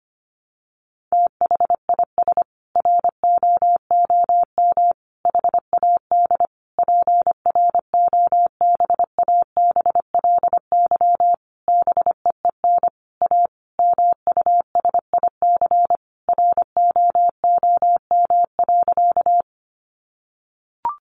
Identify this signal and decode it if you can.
Morse code, heard as a keyed tone, about 25 wpm, 700 Hz. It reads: THIS ROOM HAD PROBABLY BEEN A MUSIC ROOM.
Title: Sentences from Top 700 Words 25wpm